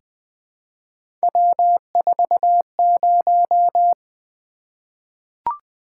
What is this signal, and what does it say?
Morse code, heard as a keyed tone, about 20 wpm, 700 Hz. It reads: W40
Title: Three Letter-Number Combinations - 20wpm